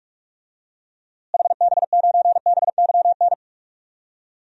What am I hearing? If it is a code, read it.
Morse code, heard as a keyed tone, about 47 wpm, 700 Hz. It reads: HB9BYN
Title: DXCC-100 Call Signs - 6 Characters - Repeated 2x - 45wpm